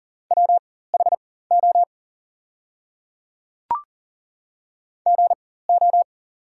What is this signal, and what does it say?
Morse code, heard as a keyed tone, about 38 wpm, spaced as for 18 wpm, 690 Hz. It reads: WHO GO
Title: 4x Character Spacing - Top 100 Words - 40wpm